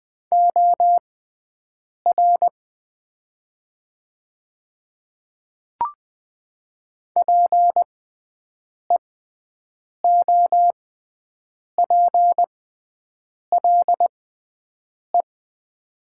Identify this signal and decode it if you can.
Morse code, heard as a keyed tone, about 20 wpm, spaced as for 7 wpm, 700 Hz. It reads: OR PEOPLE
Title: 6x Character Spacing - Top 100 Words - 20wpm